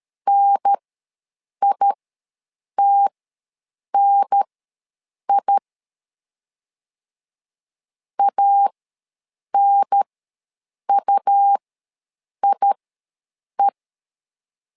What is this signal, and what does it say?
Morse code, heard as a keyed tone, about 13 wpm, spaced as for 7 wpm, 800 Hz. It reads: NITNI ANUIE
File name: LEZIONI / 2 UHDN / 2 UHDN ws 12wpm 1708